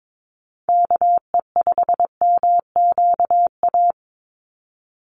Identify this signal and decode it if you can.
Morse code, heard as a keyed tone, about 22 wpm, 700 Hz. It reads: KE5MQA
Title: USA Call Signs 22wpm